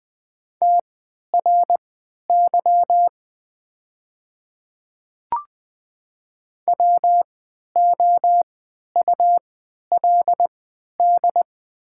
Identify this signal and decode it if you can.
Morse code, heard as a keyed tone, about 20 wpm, spaced as for 11 wpm, 700 Hz. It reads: TRY WOULD
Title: 3x Character Spacing - Top 200 Words - 20wpm